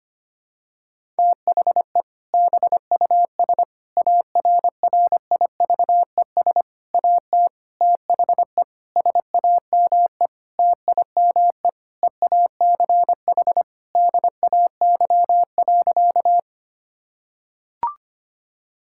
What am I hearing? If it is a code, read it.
Morse code, heard as a keyed tone, about 25 wpm, 700 Hz. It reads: THE BUS ARRIVES AT THE SAME TIME EACH DAY.
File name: practice / 25wpm / Difficult Sentences with Hints - 25wpm